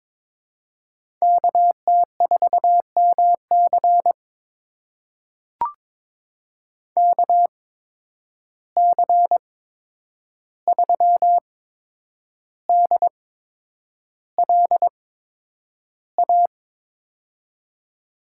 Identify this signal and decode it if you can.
Morse code, heard as a keyed tone, about 22 wpm, spaced as for 9 wpm, 700 Hz. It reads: KT4MC KC3DLA
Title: USA Call Signs - 8x Character Spacing - 22wpm